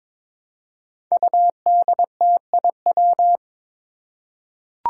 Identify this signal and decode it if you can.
Morse code, heard as a keyed tone, about 22 wpm, 700 Hz. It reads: UDTIW
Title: Five Challenging Letter-Number Combinations 22wpm